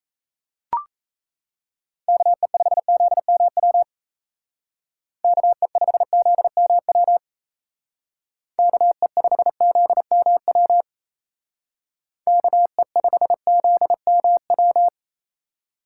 Morse code, heard as a keyed tone, about 34 wpm, 700 Hz. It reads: KE5ZMW KE5ZMW KE5ZMW KE5ZMW
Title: Speed Racing - USA 2x3 Call Signs 28wpm